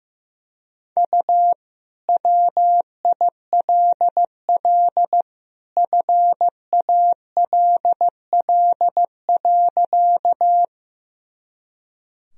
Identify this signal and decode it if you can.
Morse code, heard as a keyed tone, about 16 wpm, 700 Hz. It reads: U WILL FALL.
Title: Sentences from Top 500 Words and Abbreviations 15wpm